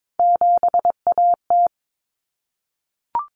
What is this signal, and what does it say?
Morse code, heard as a keyed tone, about 22 wpm, 700 Hz. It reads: 7AT
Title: Common Cut Numbers - 3-Digit 22wpm